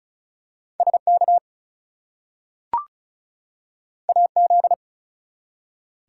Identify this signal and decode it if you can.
Morse code, heard as a keyed tone, about 36 wpm, 700 Hz. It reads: SK AZ
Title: US States and Canadian Provinces 35wpm